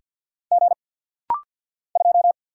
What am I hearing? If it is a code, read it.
Morse code, heard as a keyed tone, about 50 wpm, 700 Hz. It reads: G 2
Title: Single Letter-Number Alt - Rapid-Fire 50wpm